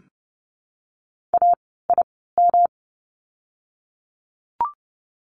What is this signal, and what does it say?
Morse code, heard as a keyed tone, about 30 wpm, spaced as for 17 wpm, 700 Hz. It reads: AIM
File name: course / 30:10wpm / 108 Lesson 18f - Letter m - Words 30wpm